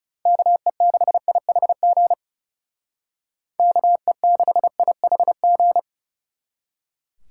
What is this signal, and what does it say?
Morse code, heard as a keyed tone, about 34 wpm, 700 Hz. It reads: KE6IHG KE6IHG
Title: Speed Racing - USA Call Signs 30wpm